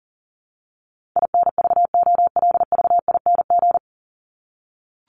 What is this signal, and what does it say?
Morse code, heard as a keyed tone, about 40 wpm, 700 Hz. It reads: INVOLVING
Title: ING Suffix - Top 100 Words with Base Words 40wpm